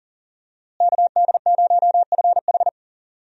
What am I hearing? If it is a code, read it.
Morse code, heard as a keyed tone, about 40 wpm, 700 Hz. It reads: KD0FH
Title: USA Call Signs - 5 Characters - Repeated 2x - 40wpm